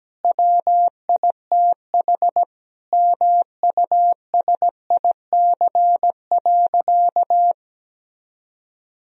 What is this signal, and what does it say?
Morse code, heard as a keyed tone, about 17 wpm, 700 Hz. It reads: WITH MUSIC.
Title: Sentences from Top 1000 Words - Review Part 1 - 17wpm